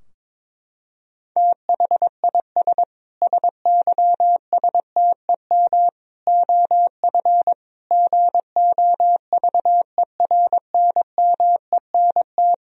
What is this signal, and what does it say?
Morse code, heard as a keyed tone, about 22 wpm, 700 Hz. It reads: THIS SYSTEM OF GOVERNMENT
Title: Sentences from Top 1000 Words - Review Part 3 - 22wpm